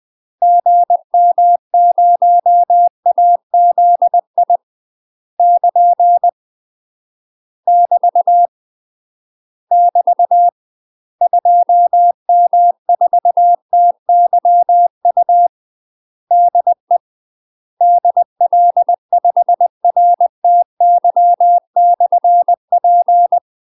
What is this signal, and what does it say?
Morse code, heard as a keyed tone, about 20 wpm, 700 Hz. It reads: GM0AZI <KN> <BT> <BT> 2M4TYU DE DL5RTY/P
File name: UK MASTER QSOs TRAINING / 20WPM / UK MASTER QSOs 0041